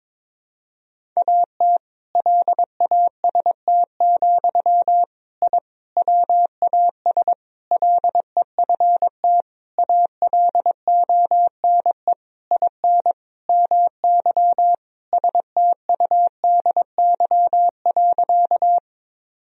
Morse code, heard as a keyed tone, about 22 wpm, 700 Hz. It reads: AT LAST, I WAS LEFT ALONE IN MY STUDY.
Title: Sentences from Top 1000 Words - Review Part 1 - 22wpm